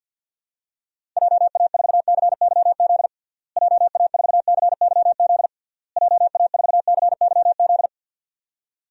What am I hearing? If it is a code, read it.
Morse code, heard as a keyed tone, about 50 wpm, 700 Hz. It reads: JA4CYZ JA4CYZ JA4CYZ
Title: DXCC-100 Call Signs - 6 Characters - Repeated 3x - 50wpm